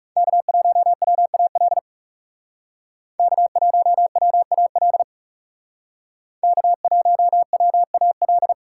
Morse code, heard as a keyed tone, about 37 wpm, 700 Hz. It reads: K1WAL K1WAL K1WAL
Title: Speed Racing - USA Call Signs 30wpm